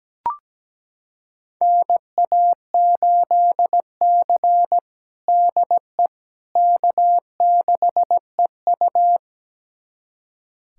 Morse code, heard as a keyed tone, about 17 wpm, 700 Hz. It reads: NA8C DE K6EU
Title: USA Callsign DE Callsign - 4 Characters - 17wpm